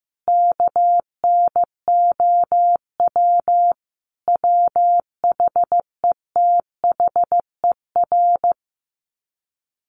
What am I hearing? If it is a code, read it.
Morse code, heard as a keyed tone, about 14 wpm, 700 Hz. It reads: KNOW WHETHER
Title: Sets of 5 Words 15wpm